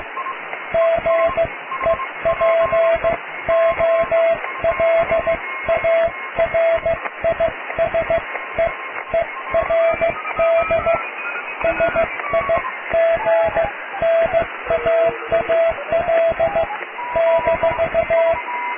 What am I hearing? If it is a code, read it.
Morse code, heard as a keyed tone, about 15 wpm, spaced as for 13 wpm, 660 Hz, with radio noise. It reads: GEPOLARISEERD SIGNAAL -
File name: cw / tekst(a)+QRM